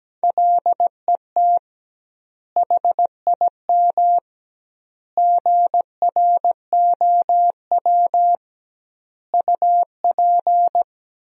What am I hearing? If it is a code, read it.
Morse code, heard as a keyed tone, about 17 wpm, 700 Hz. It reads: LET HIM GROW UP
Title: 2x Word-Spacing - Sentences from Top 300 Words - 17wpm